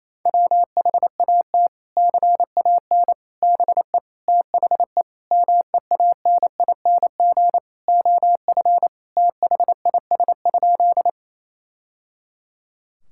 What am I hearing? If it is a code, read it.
Morse code, heard as a keyed tone, about 28 wpm, 700 Hz. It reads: WHAT CAN BE THE MEANING OF THIS?
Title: Sentences from Top 1000 Words - Review Part 6 - 28wpm